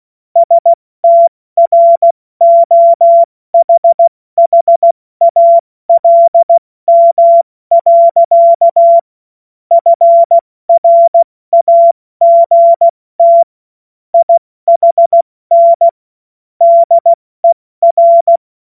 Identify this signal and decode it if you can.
Morse code, heard as a keyed tone, about 16 wpm, 680 Hz. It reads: STROHHALM. FRAGT IHN DER
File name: Morsekurs / Witze / Witz 02 Januar 2024 70 F80